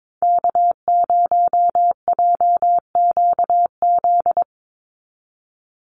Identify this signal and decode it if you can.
Morse code, heard as a keyed tone, about 22 wpm, 700 Hz. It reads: K0JQZ
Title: NA SOTA Activator Callsigns 22wpm